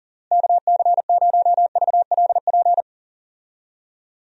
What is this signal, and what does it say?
Morse code, heard as a keyed tone, about 37 wpm, 700 Hz. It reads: KC0VLP
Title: USA Call Signs - 6 Characters - Repeated 2x - 40wpm